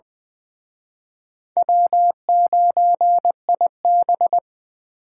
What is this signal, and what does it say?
Morse code, heard as a keyed tone, about 24 wpm, 700 Hz. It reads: W9IB
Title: Speed Racing - USA 1x2 Call Signs 20wpm